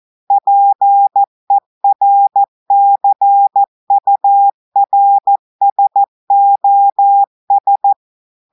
Morse code, heard as a keyed tone, about 14 wpm, 800 Hz. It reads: PERCURSOS
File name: audios / portugais / portugais texte en clair 1 vitesse 14 mots minutes